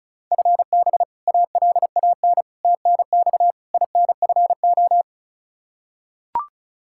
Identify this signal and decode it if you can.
Morse code, heard as a keyed tone, about 34 wpm, 700 Hz. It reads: FB ALAN TNX INFO
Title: Practice QSO - Exchange 3 - More Info - 35wpm